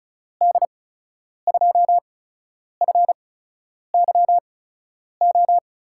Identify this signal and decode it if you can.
Morse code, heard as a keyed tone, about 36 wpm, spaced as for 10 wpm, 700 Hz. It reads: D2FYO
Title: USA Call Signs - 6 Characters - 8x Character Spacing - 35wpm